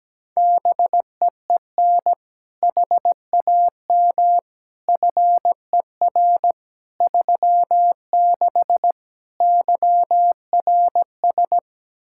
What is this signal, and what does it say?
Morse code, heard as a keyed tone, about 17 wpm, 700 Hz. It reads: BEEN HAM FER 36 YRS
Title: Practice QSO - Exchange 3 - More Info - 17wpm